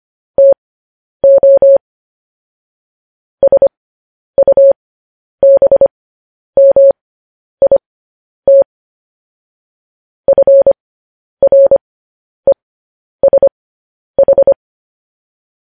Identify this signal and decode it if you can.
Morse code, heard as a keyed tone, about 25 wpm, spaced as for 10 wpm, 550 Hz. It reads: TO SUBMIT FRESH
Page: news headlines in morse code at 10 wpm